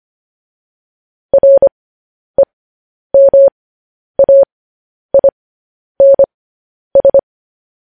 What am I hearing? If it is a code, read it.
Morse code, heard as a keyed tone, about 25 wpm, spaced as for 10 wpm, 550 Hz. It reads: REMAINS
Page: News Headlines in Morse Code at 10 WPM